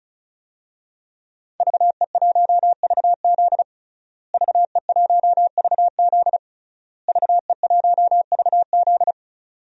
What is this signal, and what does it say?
Morse code, heard as a keyed tone, about 34 wpm, 700 Hz. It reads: VE1VZ VE1VZ VE1VZ
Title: DXCC-100 Call Signs - 5 Characters - Repeated 3x - 35wpm